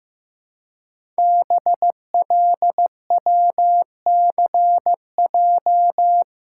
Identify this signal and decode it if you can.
Morse code, heard as a keyed tone, about 15 wpm, 700 Hz. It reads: BLWCJ
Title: Code Groups - 5 Letters - 15wpm